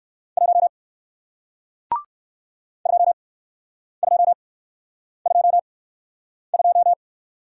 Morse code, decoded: 1 2222

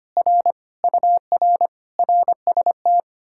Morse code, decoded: R UR RST

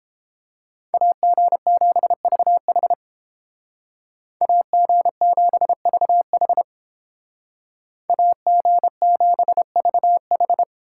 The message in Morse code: AG7VH AG7VH AG7VH

